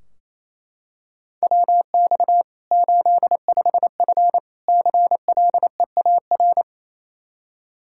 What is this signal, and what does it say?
Morse code, heard as a keyed tone, about 28 wpm, 700 Hz. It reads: WX 85F CLEAR
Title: Practice QSO - Exchange 2 - Info - 28wpm